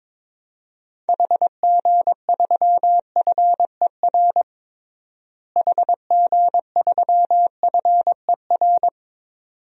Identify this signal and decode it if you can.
Morse code, heard as a keyed tone, about 22 wpm, 700 Hz. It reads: HG3FER HG3FER
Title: DXCC-100 Call Signs - 6 Characters - Repeated 2x - 22wpm